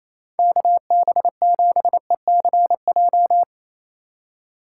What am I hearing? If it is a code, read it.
Morse code, heard as a keyed tone, about 28 wpm, 700 Hz. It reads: KB7ECJ
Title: USA Call Signs - 6 Characters - 2x Character Spacing - 28wpm